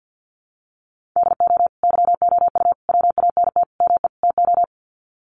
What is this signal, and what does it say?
Morse code, heard as a keyed tone, about 50 wpm, 700 Hz. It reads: DO YOU WANT ME TO
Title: Sets of 5 Words 50wpm